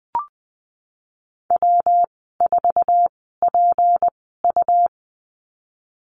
Morse code, heard as a keyed tone, about 20 wpm, spaced as for 15 wpm, 700 Hz. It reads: W4PU
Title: 161 Lesson 26g - Letter g - Call Signs 20wpm